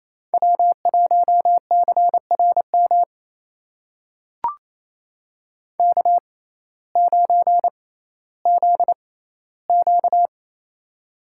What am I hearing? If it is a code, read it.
Morse code, heard as a keyed tone, about 28 wpm, spaced as for 15 wpm, 700 Hz. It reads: W1CRM K9ZQ
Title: USA Call Signs - 6x Character Spacing - 28wpm